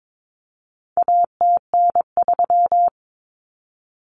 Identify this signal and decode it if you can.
Morse code, heard as a keyed tone, about 22 wpm, 700 Hz. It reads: ATN3
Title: Common Cut Numbers - 4-Digit 22wpm